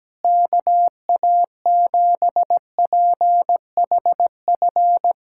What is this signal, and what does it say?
Morse code, heard as a keyed tone, about 17 wpm, 700 Hz. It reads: KA7PHF
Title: USA Callsign DE Callsign 17wpm